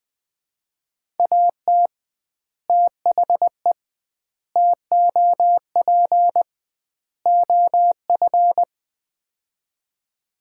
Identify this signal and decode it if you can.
Morse code, heard as a keyed tone, about 20 wpm, 700 Hz. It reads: AT THE TOP OF
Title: 2x Word-Spacing - Sets of 4 Words - 20wpm